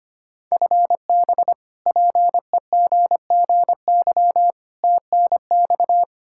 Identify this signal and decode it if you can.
Morse code, heard as a keyed tone, about 25 wpm, 700 Hz. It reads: FB PEGGY TNX
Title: Practice QSO - Exchange 3 - More Info - 25wpm